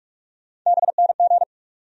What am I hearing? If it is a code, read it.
Morse code, heard as a keyed tone, about 47 wpm, 700 Hz. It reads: BNG
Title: USA Call Sign Postfixes - 3 Letters 45wpm